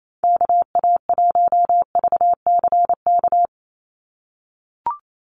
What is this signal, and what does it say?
Morse code, heard as a keyed tone, about 28 wpm, 700 Hz. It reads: KA1VCK
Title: USA Call Signs 28wpm